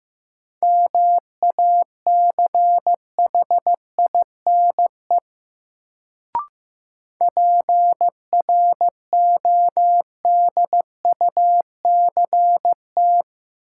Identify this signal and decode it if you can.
Morse code, heard as a keyed tone, about 15 wpm, 700 Hz. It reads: MACHINE PRODUCT
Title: Sentences from Top 1000 Words - Review Part 2 - New Word Drill - 15wpm